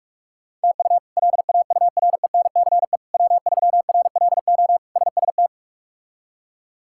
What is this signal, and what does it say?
Morse code, heard as a keyed tone, about 43 wpm, 700 Hz. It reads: TU LAURENCE W3RLO SST
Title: K1USN SST QSO Practice - 45wpm